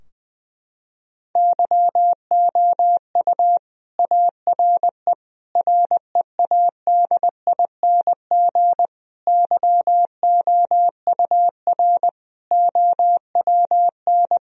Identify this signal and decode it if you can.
Morse code, heard as a keyed tone, about 20 wpm, 700 Hz. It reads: YOU ARE READING YOUR OWN
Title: Sentences from Top 1000 Words - Review Part 3 - 20wpm